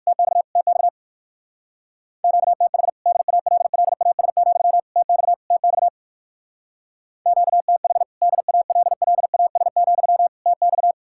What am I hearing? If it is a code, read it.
Morse code, heard as a keyed tone, about 65 wpm, spaced as for 57 wpm, 690 Hz. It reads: TX TX QTH DALLAS, TX TX QTH DALLAS, TX